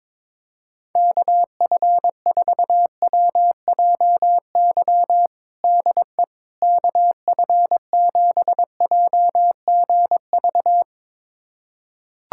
Morse code, heard as a keyed tone, about 22 wpm, 700 Hz. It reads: KF4WJY DE KF7JGV
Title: USA Callsign DE Callsign 22wpm